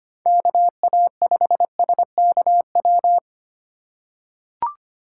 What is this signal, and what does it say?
Morse code, heard as a keyed tone, about 25 wpm, 700 Hz. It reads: KA5SKW
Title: USA Call Signs - 6 Characters - 4x Character Spacing - 25wpm